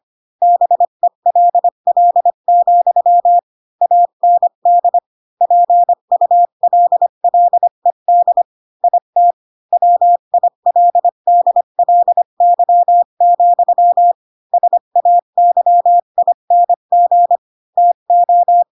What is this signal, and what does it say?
Morse code, heard as a keyed tone, about 25 wpm, 700 Hz. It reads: BELL, AND PULLED IT WILDLY, SAYING TO